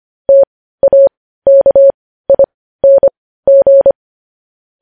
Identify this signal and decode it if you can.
Morse code, heard as a keyed tone, about 25 wpm, spaced as for 15 wpm, 550 Hz. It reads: TAKING
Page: news headlines in morse code at 15 wpm